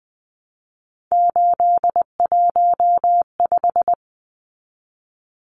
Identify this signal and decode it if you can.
Morse code, heard as a keyed tone, about 20 wpm, 700 Hz. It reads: 815
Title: Numbers - 3-Digit 20wpm